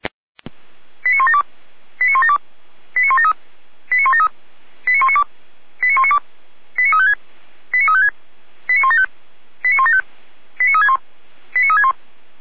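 Five-tone selective call (ZVEI), heard as quick runs of five tones, tones of about 70 ms. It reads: 89172 89172 89173 89173 89192 89192 89356 89356 89175 89175 89361 89361
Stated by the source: ZVEI five-tone call received by SDR, from the audio server of feuerwehr-orlamuende.org (fire brigade Orlamünde).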